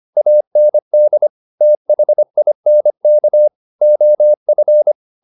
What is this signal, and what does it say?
Morse code, heard as a keyed tone, about 25 wpm, 600 Hz. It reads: AND THINK OF
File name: the old man and the sea part 1 25wpm